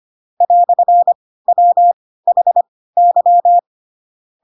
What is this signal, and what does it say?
Morse code, heard as a keyed tone, about 25 wpm, spaced as for 16 wpm, 700 Hz. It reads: "WHY
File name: Pinocchio / Pinocchio ch7 10 25 ik1hge